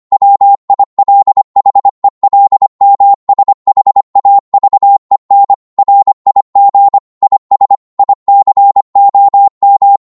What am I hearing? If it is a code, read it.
Morse code, heard as a keyed tone, about 25 wpm, 800 Hz. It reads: WILHELMSHAVEN RIG IS ICOM
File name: Morse / qso-25wpm